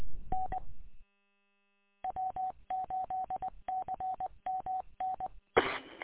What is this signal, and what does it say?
Morse code, heard as a keyed tone, about 20 wpm, 750 Hz. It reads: N W8CMN E
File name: w8cmn- -combined